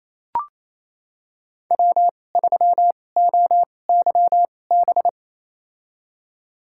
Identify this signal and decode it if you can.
Morse code, heard as a keyed tone, about 28 wpm, spaced as for 20 wpm, 700 Hz. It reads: W3OYB
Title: USA Call Signs - 5 Characters - 2x Character Spacing - 28wpm